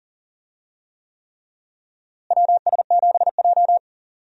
Morse code, heard as a keyed tone, about 40 wpm, 700 Hz. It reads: WS7J
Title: USA Call Signs - 4 Characters - Repeated 3x - 40wpm